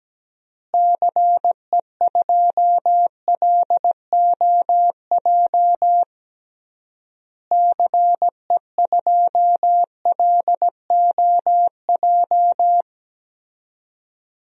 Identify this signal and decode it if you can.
Morse code, heard as a keyed tone, about 17 wpm, 700 Hz. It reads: CE2LOJ CE2LOJ
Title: DXCC-100 Call Signs - 6 Characters - Repeated 3x - 17wpm